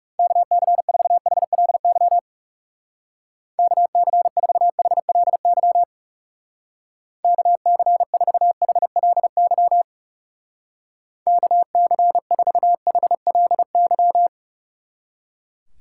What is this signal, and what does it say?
Morse code, heard as a keyed tone, about 38 wpm, 700 Hz. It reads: KC4HLY KC4HLY KC4HLY KC4HLY